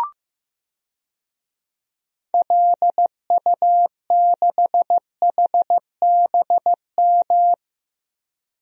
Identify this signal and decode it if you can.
Morse code, heard as a keyed tone, about 15 wpm, 700 Hz. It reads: LU6HBM
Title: DXCC-100 Call Signs - 6 Characters - Repeated 2x - 15wpm